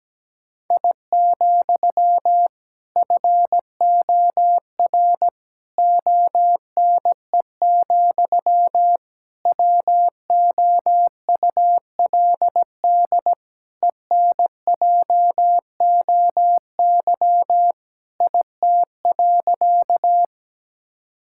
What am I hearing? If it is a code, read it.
Morse code, heard as a keyed tone, about 17 wpm, 700 Hz. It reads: I, FOR ONE, WOULD ENJOY IT.
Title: Sentences from Top 1000 Words - Review Part 5 - 17wpm